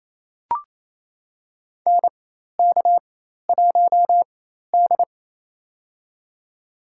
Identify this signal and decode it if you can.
Morse code, heard as a keyed tone, about 27 wpm, spaced as for 13 wpm, 700 Hz. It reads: NK1D
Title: USA Call Signs - 4 Characters - 4x Character Spacing - 28wpm